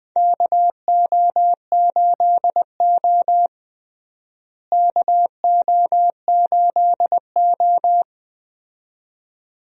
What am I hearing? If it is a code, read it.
Morse code, heard as a keyed tone, about 20 wpm, 700 Hz. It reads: KO8O KO8O
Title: USA Call Signs - 4 Characters - Repeated 2x - 20wpm